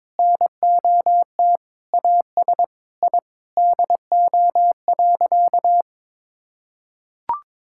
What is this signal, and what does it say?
Morse code, heard as a keyed tone, about 22 wpm, 700 Hz. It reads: NOT AS I DO.
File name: Sentences with 7-Word Phrases - 22wpm